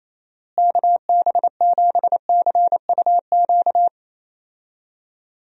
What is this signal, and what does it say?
Morse code, heard as a keyed tone, about 28 wpm, 700 Hz. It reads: KB7CUQ
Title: USA Call Signs - 6 Characters - Repeated 3x - 28wpm